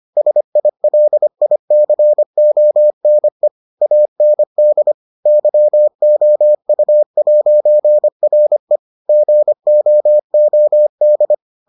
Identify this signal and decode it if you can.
Morse code, heard as a keyed tone, about 25 wpm, 600 Hz. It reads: SILICONE AND YOU'RE GOOD